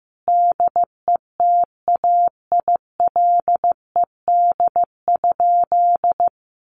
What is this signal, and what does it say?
Morse code, heard as a keyed tone, about 15 wpm, 700 Hz. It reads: DETAILED?
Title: Sentences from Top 1500 Words 15wpm